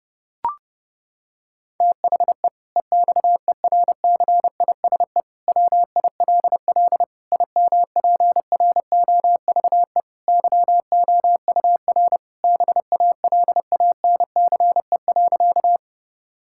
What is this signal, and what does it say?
Morse code, heard as a keyed tone, about 30 wpm, 700 Hz. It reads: THE EXERCISE WILL IMPROVE YOUR BALANCE.
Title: Sentences from Top 1000 Words - Review Part 6 - 30wpm